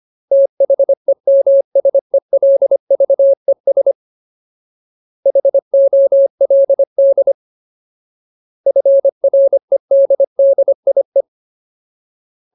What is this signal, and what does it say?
Morse code, heard as a keyed tone, about 25 wpm, 550 Hz. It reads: THEMSELVES HOLD FREDDIE
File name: Ultimate Word List / 25WPM Ultimate